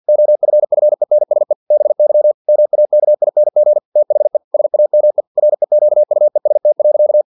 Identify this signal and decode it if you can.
Morse code, heard as a keyed tone, about 50 wpm, 600 Hz. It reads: OFFENSE BY MAKING THE SAME REQUEST.
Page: Never 49wpm